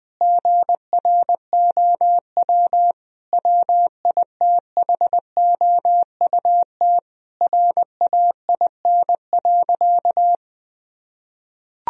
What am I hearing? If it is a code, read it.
Morse code, heard as a keyed tone, about 20 wpm, 700 Hz. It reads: GROW WITHOUT RAIN.
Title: Sentences from Top 1000 Words - Review Part 3 - 20wpm